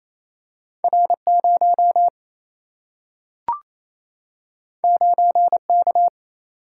Morse code, heard as a keyed tone, about 28 wpm, 700 Hz. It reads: R0 9K
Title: Single-Letter Single-Number 28wpm